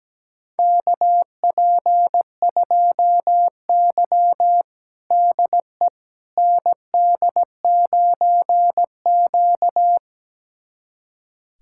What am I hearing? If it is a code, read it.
Morse code, heard as a keyed tone, about 17 wpm, 700 Hz. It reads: KP2Y DE ND9Q
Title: USA Callsign DE Callsign - 4 Characters - 17wpm